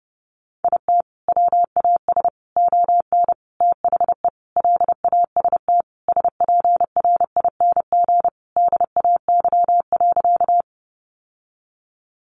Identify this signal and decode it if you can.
Morse code, heard as a keyed tone, about 30 wpm, 700 Hz. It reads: IT WAS ON THE LAST SPRING DAY.
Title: Sentences from Top 1200 Words 30wpm